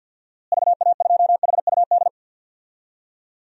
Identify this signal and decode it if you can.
Morse code, heard as a keyed tone, about 47 wpm, 700 Hz. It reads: VA2HUD
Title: DXCC-100 Call Signs - 6 Characters - Repeated 2x - 50wpm